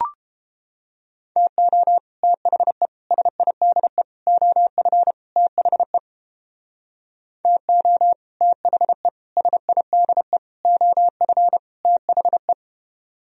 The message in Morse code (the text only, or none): TO THE SIDE OF THE TO THE SIDE OF THE